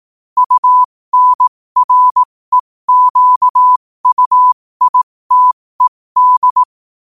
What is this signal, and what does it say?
Morse code, heard as a keyed tone, about 18 wpm, spaced as for 15 wpm, 1000 Hz. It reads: UNREQUITED
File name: qotdfiles / 15wpm